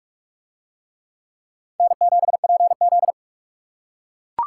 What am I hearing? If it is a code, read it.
Morse code, heard as a keyed tone, about 41 wpm, 700 Hz. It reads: N7PZ